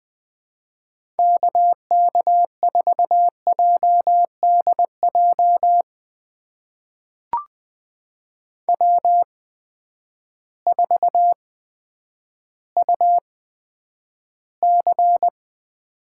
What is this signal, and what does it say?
Morse code, heard as a keyed tone, about 20 wpm, spaced as for 10 wpm, 700 Hz. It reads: KK4JDJ W4UC
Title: USA Call Signs - 8x Character Spacing - 20wpm